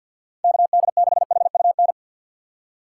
Morse code, heard as a keyed tone, about 47 wpm, 700 Hz. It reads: KD6HUN
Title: USA Call Signs - 6 Characters - 3x Character Spacing - 50wpm